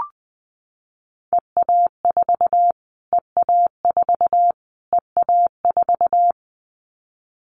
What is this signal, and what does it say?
Morse code, heard as a keyed tone, about 20 wpm, 700 Hz. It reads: EA4 EA4 EA4
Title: Callsign Prefixes for Top 25 DXCC - Repeated 3x - 20wpm